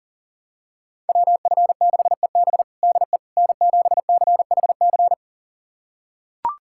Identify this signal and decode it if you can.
Morse code, heard as a keyed tone, about 40 wpm, 700 Hz. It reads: WF6EB DE N7CHC
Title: USA Callsign DE Callsign - 5 Characters - 40wpm